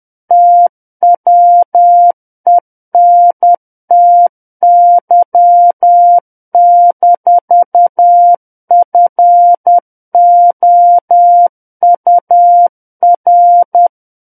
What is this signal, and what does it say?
Morse code, heard as a keyed tone, about 10 wpm, 700 Hz. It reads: TWENTY-FOUR
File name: Book One / 01-Chapter 1 - 10 WPM